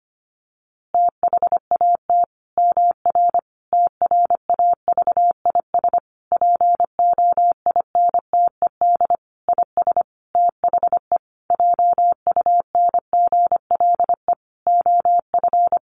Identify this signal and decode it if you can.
Morse code, heard as a keyed tone, about 25 wpm, 700 Hz. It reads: THAT MR TRAVIS POINTED IS THE JUNGLE OF
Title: A Sound Of Thunder 25wpm